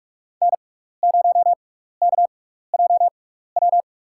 Morse code, heard as a keyed tone, about 46 wpm, spaced as for 15 wpm, 700 Hz. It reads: N0KJW
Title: USA Call Signs - 5 Characters - 6x Character Spacing - 45wpm